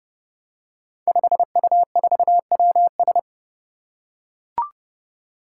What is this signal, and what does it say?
Morse code, heard as a keyed tone, about 30 wpm, 700 Hz. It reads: 5U4WS
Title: Five Challenging Letter-Number Combinations 30wpm